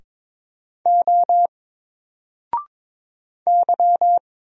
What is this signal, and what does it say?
Morse code, heard as a keyed tone, about 22 wpm, 700 Hz. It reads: O Y
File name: Single-Letter-Number-22wpm